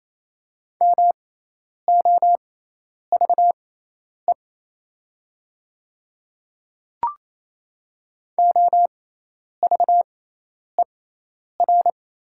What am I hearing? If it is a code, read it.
Morse code, heard as a keyed tone, about 28 wpm, spaced as for 10 wpm, 700 Hz. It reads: MOVE OVER
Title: 6x Character Spacing - Top 300 Words - 28wpm